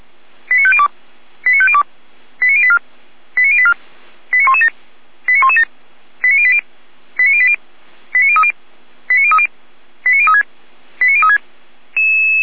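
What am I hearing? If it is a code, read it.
Five-tone selective call (ZVEI), heard as quick runs of five tones, tones of about 70 ms. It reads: 89502 89502 89084 89084 89117 89117 89080 89080 89030 89030 89036 89036
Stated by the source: ZVEI five-tone selective call 89120, received by SDR, feuerwehr-orlamuende.org fire brigade.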